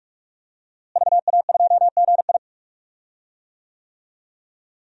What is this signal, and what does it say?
Morse code, heard as a keyed tone, about 46 wpm, 700 Hz. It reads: VA2GI
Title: DXCC-100 Call Signs - 5 Characters - Repeated 2x - 45wpm